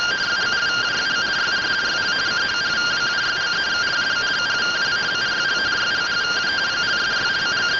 Two-tone teleprinter signal, about 170 Hz shift.